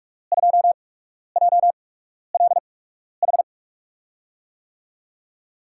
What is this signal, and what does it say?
Morse code, heard as a keyed tone, about 47 wpm, spaced as for 12 wpm, 700 Hz. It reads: 2JLH
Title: USA Call Signs - 5 Characters - 8x Character Spacing - 45wpm